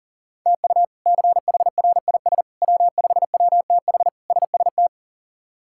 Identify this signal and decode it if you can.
Morse code, heard as a keyed tone, about 40 wpm, 700 Hz. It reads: TU CHRIS W5WTH SST
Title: K1USN SST QSO Practice - 40wpm